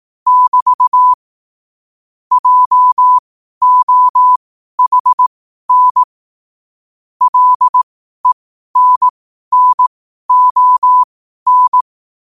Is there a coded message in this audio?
Yes: Morse code from 0.3 to 11.8 s, heard as a keyed tone, about 18 wpm, spaced as for 12 wpm, 1000 Hz. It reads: <BT> JOHN LENNON